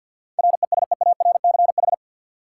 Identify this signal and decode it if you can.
Morse code, heard as a keyed tone, about 51 wpm, 700 Hz. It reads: RESEARCH
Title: Sentences from Top 1000 Words - Review Part 4 - New Word Drill - 50wpm